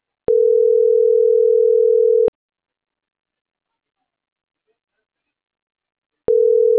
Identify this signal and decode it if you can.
Ringback tone (North America), 440 + 480 Hz.